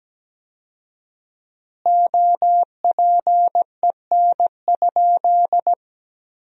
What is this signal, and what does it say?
Morse code, heard as a keyed tone, about 17 wpm, 700 Hz. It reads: OPEN?
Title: 4x Word-Spacing - Sentences from Top 300 Words - 17wpm